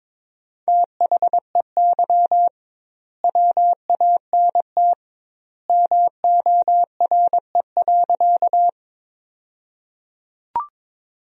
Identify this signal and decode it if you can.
Morse code, heard as a keyed tone, about 22 wpm, 700 Hz. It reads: THEY WANT MORE.
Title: 2x Word-Spacing - Sentences from Top 200 Words - 22wpm